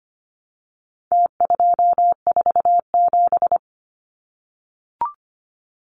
Morse code, heard as a keyed tone, about 25 wpm, 700 Hz. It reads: T247
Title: Numbers - Contest Style - 4-Digit 25wpm